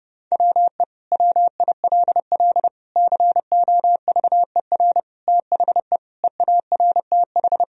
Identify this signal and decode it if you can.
Morse code, heard as a keyed tone, about 30 wpm, 700 Hz. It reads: WE WILL COVER THE EARTH
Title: Sentences from Top 1000 Words - Review Part 1 - 30wpm